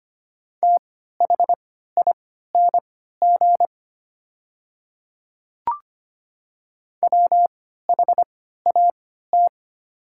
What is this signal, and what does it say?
Morse code, heard as a keyed tone, about 25 wpm, spaced as for 14 wpm, 700 Hz. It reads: THING WHAT
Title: 3x Character Spacing - Top 200 Words - 25wpm